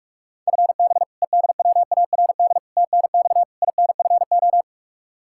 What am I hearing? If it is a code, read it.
Morse code, heard as a keyed tone, about 46 wpm, 700 Hz. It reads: FB EDWARD TNX INFO
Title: Practice QSO - Exchange 3 - More Info - 45wpm